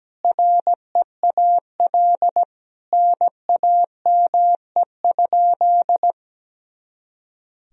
Morse code, heard as a keyed tone, about 17 wpm, 700 Hz. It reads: REAL NAME?
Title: Sentences from Top 300 Words and Abbreviations 17wpm